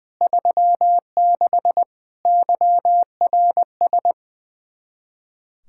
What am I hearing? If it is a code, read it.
Morse code, heard as a keyed tone, about 21 wpm, 700 Hz. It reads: 36 YRS
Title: K1USN SST QSO Practice 20wpm